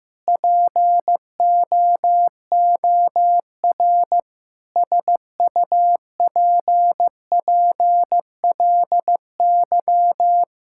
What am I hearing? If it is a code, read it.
Morse code, heard as a keyed tone, about 15 wpm, 700 Hz. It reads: POOR SUPPLY
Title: Sentences from Top 1000 Words - Review Part 5 - 15wpm